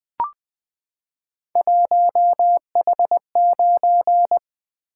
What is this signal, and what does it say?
Morse code, heard as a keyed tone, about 20 wpm, 700 Hz. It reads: 1H9
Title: Three Letter-Number Combinations - 20wpm